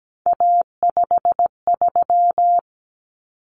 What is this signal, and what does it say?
Morse code, heard as a keyed tone, about 17 wpm, 700 Hz. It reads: A53
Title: Common Cut Numbers - 3-Digit 17wpm